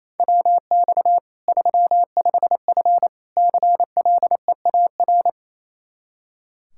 Morse code, heard as a keyed tone, about 27 wpm, 700 Hz. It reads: WX 35F CLEAR
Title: K1USN SST QSO Practice 28wpm